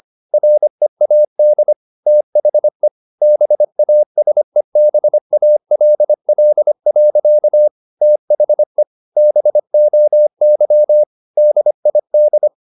Morse code, heard as a keyed tone, about 25 wpm, 600 Hz. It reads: READ THE BASEBALL. THE BOY DID
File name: the old man and the sea part 1 25wpm